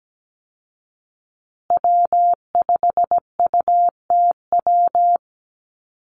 Morse code, heard as a keyed tone, about 17 wpm, 700 Hz. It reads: W5UTW
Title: Call Signs Repeated 2x - 17wpm